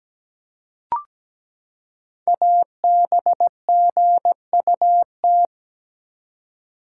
Morse code, heard as a keyed tone, about 17 wpm, 700 Hz. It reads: ABGUT
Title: Code Groups - 5 Letters - 17wpm